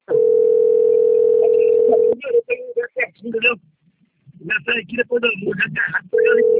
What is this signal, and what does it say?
Ringback tone (North America), 440 + 480 Hz, with speech.
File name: out- - - - - .630